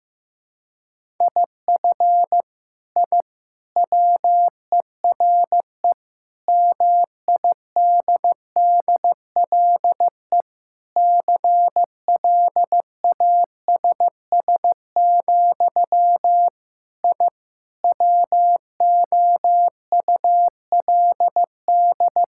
Morse code, heard as a keyed tone, about 15 wpm, 700 Hz. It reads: IF I WERE MIDDLE CLASS, I WOULD
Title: Sentences from Top 1000 Words - Review Part 6 - 15wpm